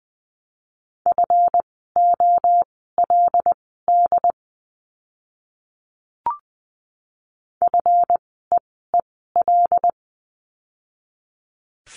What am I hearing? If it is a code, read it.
Morse code, heard as a keyed tone, about 20 wpm, spaced as for 14 wpm, 700 Hz. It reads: FOLD FEEL
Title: 139 Lesson 23f - Letter f - Words 20wpm